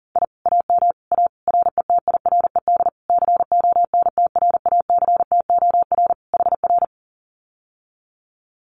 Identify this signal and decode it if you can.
Morse code, heard as a keyed tone, about 40 wpm, 700 Hz. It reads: I AM A RETIRED CONTRACTOR HR
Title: Common QSO Phrases Encore 40wpm